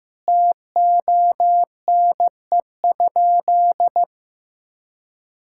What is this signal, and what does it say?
Morse code, heard as a keyed tone, about 15 wpm, 700 Hz. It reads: TONE?
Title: Sentences from Top 1600 Words 15wpm